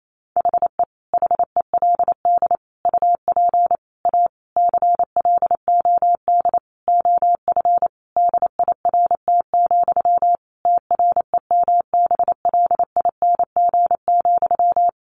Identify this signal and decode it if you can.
Morse code, heard as a keyed tone, about 28 wpm, 700 Hz. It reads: HE HELD UP A CLOD OF DIRT, TREMBLING,
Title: A Sound Of Thunder 28wpm